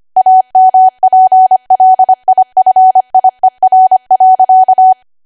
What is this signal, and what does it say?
Morse code, heard as a keyed tone, about 25 wpm, 750 Hz. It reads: AMPLIFIER.
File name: Archive / 25 WPM / 25WPM